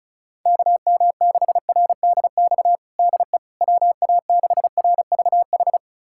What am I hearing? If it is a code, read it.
Morse code, heard as a keyed tone, about 36 wpm, 700 Hz. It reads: KM6RDX DE WA6RVH
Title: USA Callsign DE Callsign - 6 Characters - 35wpm